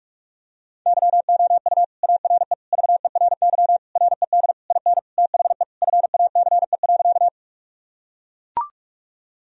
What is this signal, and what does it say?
Morse code, heard as a keyed tone, about 41 wpm, 700 Hz. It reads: YOU ARE VERY RED IN THE FACE.
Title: Sentences from Top 1000 Words - Review Part 2 - 45wpm